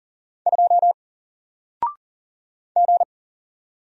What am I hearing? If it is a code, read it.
Morse code, heard as a keyed tone, about 40 wpm, 700 Hz. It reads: 2 G
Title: Single Character (Letter:Number:Punctuation:Prosign) 40wpm